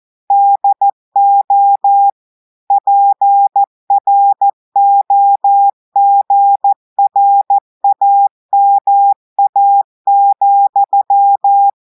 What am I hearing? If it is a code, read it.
Morse code, heard as a keyed tone, about 14 wpm, 800 Hz. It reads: DO PROGRAMA,
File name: audios / portugais / portugais texte en clair 2 vitesse 14 mots minutes